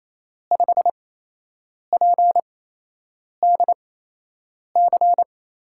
Morse code, heard as a keyed tone, about 28 wpm, spaced as for 8 wpm, 700 Hz. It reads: 5PDC
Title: USA Call Signs - 6 Characters - 8x Character Spacing - 28wpm